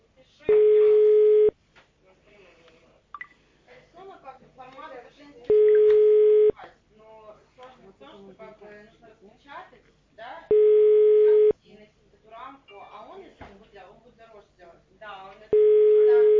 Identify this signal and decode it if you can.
Ringback tone (Europe, 425 Hz), with speech.